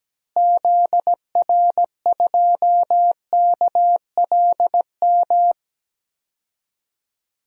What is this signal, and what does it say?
Morse code, heard as a keyed tone, about 17 wpm, 700 Hz. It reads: ZR2KLM
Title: DXCC-100 Call Signs - 6 Characters - Repeated 2x - 17wpm